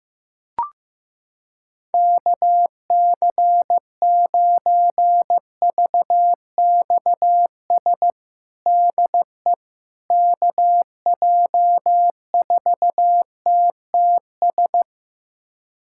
Morse code, heard as a keyed tone, about 15 wpm, 700 Hz. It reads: KC9VXS DE KJ4TTS
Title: USA Callsign DE Callsign - 6 Characters - 15wpm